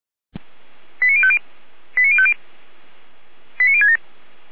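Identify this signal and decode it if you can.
Five-tone selective call (ZVEI), heard as quick runs of five tones, tones of about 70 ms. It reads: 89050 89050 89067
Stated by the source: ZVEI five-tone call received by SDR.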